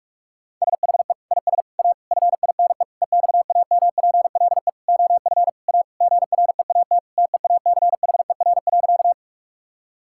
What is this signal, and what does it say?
Morse code, heard as a keyed tone, about 46 wpm, 700 Hz. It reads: SHE IS A FINE EXAMPLE OF A GREAT TEACHER.
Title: Sentences from Top 1000 Words - Review Part 6 - 45wpm